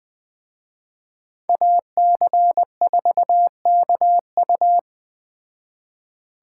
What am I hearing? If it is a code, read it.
Morse code, heard as a keyed tone, about 20 wpm, 700 Hz. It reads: AC4KU
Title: Speed Racing - USA 2x2 Call Signs 17wpm